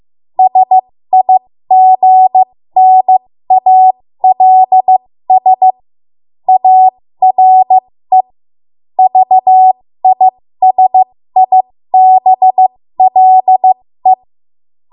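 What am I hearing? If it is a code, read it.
Morse code, heard as a keyed tone, about 15 wpm, spaced as for 13 wpm, 750 Hz. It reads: SIGNALS ARE VISIBLE
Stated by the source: 13 wpm Morse code practice archive transmission.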